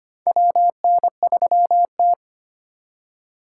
Morse code, heard as a keyed tone, about 25 wpm, 700 Hz. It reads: WN3T
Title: USA Call Signs - 4 Characters - 6x Character Spacing - 25wpm